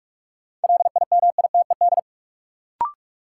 Morse code, heard as a keyed tone, about 47 wpm, 700 Hz. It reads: LIMITED
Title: Sentences from Top 900 Words - New Word Drill - 45wpm